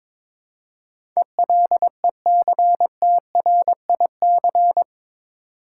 Morse code, heard as a keyed tone, about 21 wpm, 700 Hz. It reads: ELECTRIC
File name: practice / 22wpm / Eight Letter Words - 22wpm